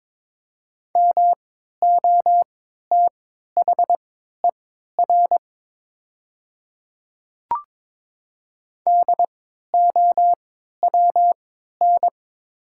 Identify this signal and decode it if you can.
Morse code, heard as a keyed tone, about 23 wpm, spaced as for 13 wpm, 700 Hz. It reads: MOTHER DOWN
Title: 3x Character Spacing - Top 200 Words - 22wpm